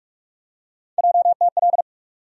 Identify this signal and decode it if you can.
Morse code, heard as a keyed tone, about 42 wpm, 700 Hz. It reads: JTL
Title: USA Call Sign Postfixes - 3 Letters 45wpm